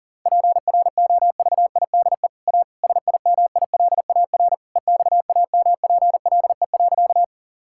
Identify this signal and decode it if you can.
Morse code, heard as a keyed tone, about 37 wpm, 700 Hz. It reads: PROVIDE A SIMILAR EXAMPLE.
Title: Sentences from Top 1000 Words - Review Part 5 - 40wpm